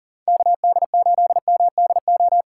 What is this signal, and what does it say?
Morse code, heard as a keyed tone, about 40 wpm, 700 Hz. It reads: KD8MDO
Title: USA Call Signs - 6 Characters - 4x Character Spacing - 40wpm